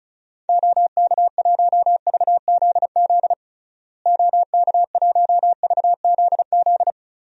Morse code, heard as a keyed tone, about 34 wpm, 700 Hz. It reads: OK1VZZ OK1VZZ
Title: DXCC-100 Call Signs - 6 Characters - Repeated 2x - 35wpm